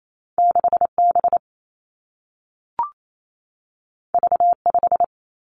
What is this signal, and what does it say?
Morse code, heard as a keyed tone, about 28 wpm, 700 Hz. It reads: 6 B V 5
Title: Two Challenging Letter-Number Combinations 28wpm